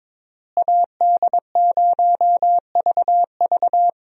Morse code, heard as a keyed tone, about 22 wpm, 700 Hz. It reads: AD0VV
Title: USA Call Signs - 5 Characters - Repeated 2x - 22wpm